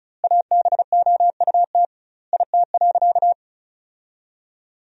Morse code, heard as a keyed tone, about 36 wpm, 700 Hz. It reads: ABOUT IT.
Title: 2x Word-Spacing - Sentences from Top 500 Words - 35wpm